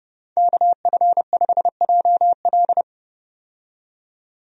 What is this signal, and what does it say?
Morse code, heard as a keyed tone, about 30 wpm, 700 Hz. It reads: KF5JL
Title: USA Call Signs - 5 Characters - 30wpm